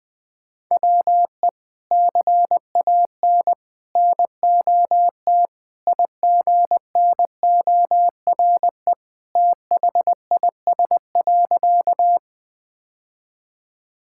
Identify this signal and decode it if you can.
Morse code, heard as a keyed tone, about 21 wpm, 700 Hz. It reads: WE CAN NOT IGNORE THIS.
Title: Sentences with 3-Word Phrases - 20wpm